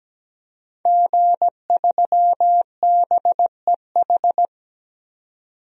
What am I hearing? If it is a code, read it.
Morse code, heard as a keyed tone, about 17 wpm, 700 Hz. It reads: G3BEH
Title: DXCC-100 Call Signs - 5 Characters - 17wpm